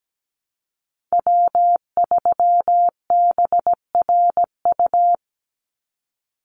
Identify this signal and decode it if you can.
Morse code, heard as a keyed tone, about 17 wpm, 700 Hz. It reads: W3BRU
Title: USA Call Signs 17wpm